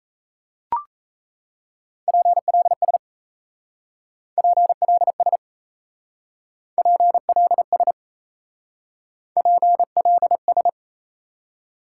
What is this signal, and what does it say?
Morse code, heard as a keyed tone, about 36 wpm, 700 Hz. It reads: PLS PLS PLS PLS